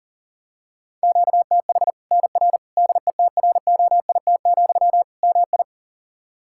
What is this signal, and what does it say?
Morse code, heard as a keyed tone, about 40 wpm, 700 Hz. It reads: QTH NR DETROIT, MI
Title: Practice QSO - Exchange 1 - Beginning - 40wpm